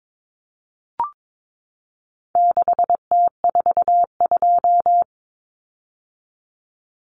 Morse code, heard as a keyed tone, about 22 wpm, 700 Hz. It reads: 6T42
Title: Common Cut Numbers - 4-Digit 22wpm